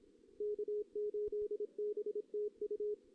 Morse code, heard as a keyed tone, about 28 wpm, 440 Hz. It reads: K8BTU
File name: 2016 JAN CW NAQP / 2016 Jan naqp k5ot - 0-2